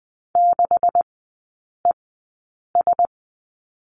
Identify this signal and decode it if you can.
Morse code, heard as a keyed tone, about 20 wpm, spaced as for 8 wpm, 700 Hz. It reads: 6ES
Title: 138 Lesson 23e - Letter f - Three Characters 20wpm